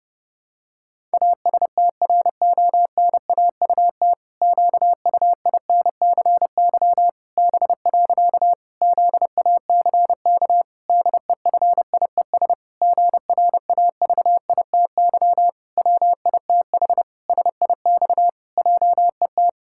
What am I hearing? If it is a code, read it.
Morse code, heard as a keyed tone, about 30 wpm, 700 Hz. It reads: ASTRONAUT QUINCY B. ZACK DEFIES GRAVITY WITH SIX JET
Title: Pangram Challenge v2 - 30wpm